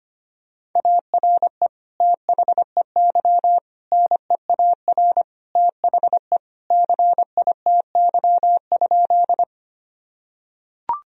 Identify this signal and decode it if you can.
Morse code, heard as a keyed tone, about 25 wpm, 700 Hz. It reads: ARE THEY NEAR THE CITY?
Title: Sentences from Top 1000 Words - Review Part 1 - 25wpm